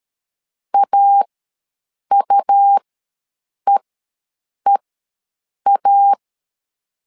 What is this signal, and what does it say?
Morse code, heard as a keyed tone, about 13 wpm, spaced as for 7 wpm, 800 Hz. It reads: AUEEA
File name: LEZIONI / 2 UHDN / 2 U ws 12wpm 1655